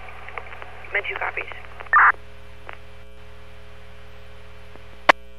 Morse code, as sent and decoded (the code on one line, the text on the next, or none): none